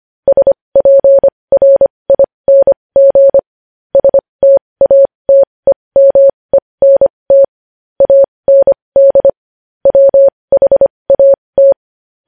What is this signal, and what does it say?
Morse code, heard as a keyed tone, about 25 wpm, spaced as for 20 wpm, 550 Hz. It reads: SPRING STATEMENT AND WHAT
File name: podcast / News headlines-1x-25x20